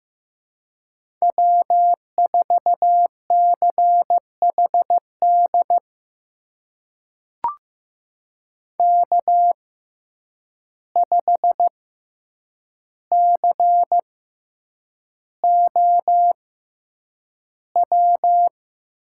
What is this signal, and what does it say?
Morse code, heard as a keyed tone, about 15 wpm, spaced as for 8 wpm, 700 Hz. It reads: W4CHD K5COW
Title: USA Call Signs - 5 Characters - 6x Character Spacing - 15wpm